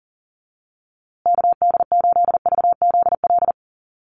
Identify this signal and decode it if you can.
Morse code, heard as a keyed tone, about 37 wpm, 700 Hz. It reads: KD8VZL
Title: USA Call Signs Repeated 3x 40wpm